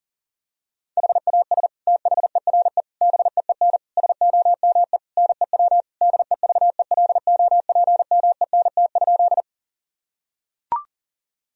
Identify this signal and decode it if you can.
Morse code, heard as a keyed tone, about 40 wpm, 700 Hz. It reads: HAS THERE BEEN SOME NEW DEVELOPMENT?
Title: Sentences from Top 1000 Words - Review Part 4 - 40wpm